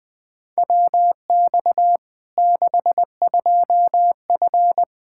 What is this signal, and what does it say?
Morse code, heard as a keyed tone, about 20 wpm, 700 Hz. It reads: WX 62F
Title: Practice QSO - Exchange 2 - Info - 20wpm